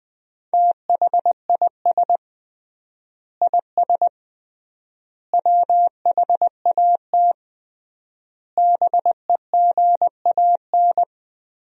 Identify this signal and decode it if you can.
Morse code, heard as a keyed tone, about 20 wpm, 700 Hz. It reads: THIS IS WHAT BEGAN
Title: 3x Word-Spacing - Sentences from Top 300 Words - 20wpm